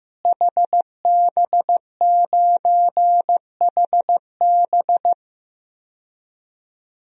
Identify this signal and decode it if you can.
Morse code, heard as a keyed tone, about 16 wpm, 700 Hz. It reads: HB9HB
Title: DXCC-100 Call Signs - 5 Characters - Repeated 2x - 15wpm